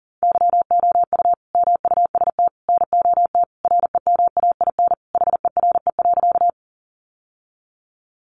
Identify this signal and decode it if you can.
Morse code, heard as a keyed tone, about 38 wpm, 700 Hz. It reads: YOU MUST NOT REMAIN HERE.
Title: Sentences from Top 1200 Words 40wpm